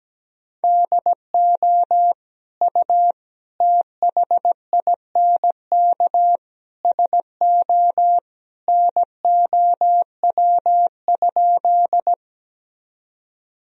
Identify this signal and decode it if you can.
Morse code, heard as a keyed tone, about 17 wpm, 700 Hz. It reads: DO U THINK SO NOW?
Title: Sentences from Top 200 Words and Abbreviations 17wpm